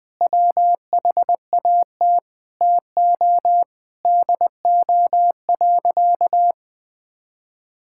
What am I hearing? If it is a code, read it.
Morse code, heard as a keyed tone, about 20 wpm, 700 Hz. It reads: WHAT TO DO.
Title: Sentences with 5-Word Phrases - 20wpm